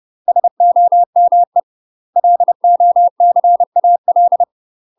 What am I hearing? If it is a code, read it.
Morse code, heard as a keyed tone, about 30 wpm, 700 Hz. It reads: SOME LOCAL
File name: UK MASTER QSOs TRAINING / 30WPM / UK MASTER QSOs 0044